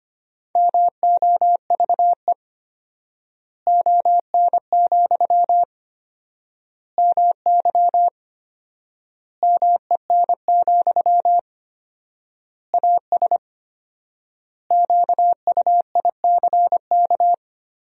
Morse code, heard as a keyed tone, about 25 wpm, 700 Hz. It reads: MOVE ON, MY MEN, AS QUICK
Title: 4x Word-Spacing - Sentences from Top 500 Words - 25wpm